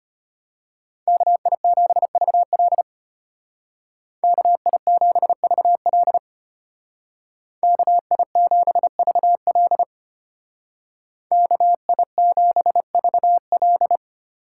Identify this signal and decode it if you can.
Morse code, heard as a keyed tone, about 33 wpm, 700 Hz. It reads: KI7VL KI7VL KI7VL KI7VL